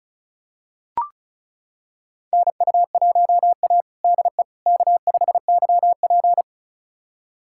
Morse code, heard as a keyed tone, about 36 wpm, 700 Hz. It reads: NU1A DE K5YP
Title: USA Callsign DE Callsign - 4 Characters - 35wpm